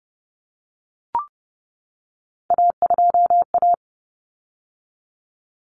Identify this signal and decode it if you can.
Morse code, heard as a keyed tone, about 30 wpm, 700 Hz. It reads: A2A
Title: Common Cut Numbers - 3-Digit 30wpm